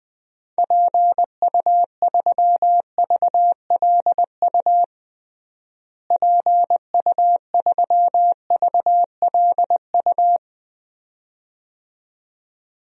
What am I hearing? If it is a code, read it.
Morse code, heard as a keyed tone, about 21 wpm, 700 Hz. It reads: PU3VLU PU3VLU
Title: DXCC-100 Call Signs - 6 Characters - Repeated 2x - 20wpm